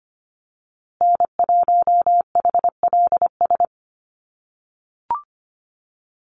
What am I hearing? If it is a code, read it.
Morse code, heard as a keyed tone, about 25 wpm, 700 Hz. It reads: N1HLS
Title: USA Call Signs Repeated 3x 25wpm